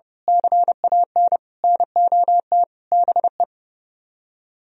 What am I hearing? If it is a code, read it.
Morse code, heard as a keyed tone, about 33 wpm, 700 Hz. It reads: CAN NOT BE
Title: Speed-Racing - Sets of 5 Words 30wpm